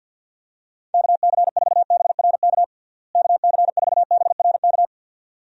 Morse code, heard as a keyed tone, about 53 wpm, 700 Hz. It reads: KC4BRK KC4BRK
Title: USA Call Signs - 6 Characters - Repeated 2x - 50wpm